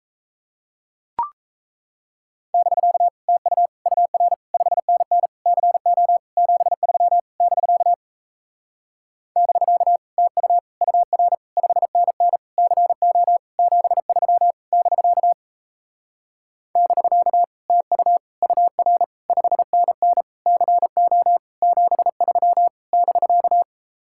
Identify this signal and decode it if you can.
Morse code, heard as a keyed tone, about 38 wpm, 700 Hz. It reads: <BK> TU UR 5NN CO 73 <BK> <BK> TU UR 5NN CO 73 <BK> <BK> TU UR 5NN CO 73 <BK>